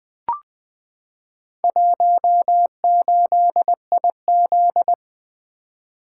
Morse code, heard as a keyed tone, about 20 wpm, 700 Hz. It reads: 18IZ
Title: Four Letter-Number Combinations - 20wpm